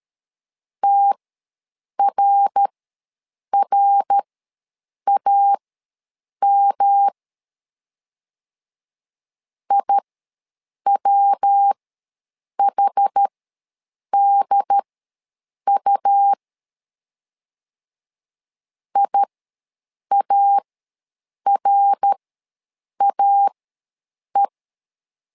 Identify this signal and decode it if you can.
Morse code, heard as a keyed tone, about 13 wpm, spaced as for 7 wpm, 800 Hz. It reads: TRRAM IWHDU IARAE
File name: LEZIONI / 3 MWRO / 3 MWR ws 12wpm 1809